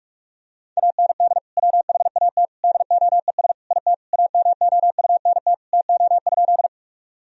Morse code, heard as a keyed tone, about 47 wpm, 700 Hz. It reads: AND WHAT DOES IT AMOUNT TO?
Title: Sentences from Top 1000 Words - Review Part 4 - 45wpm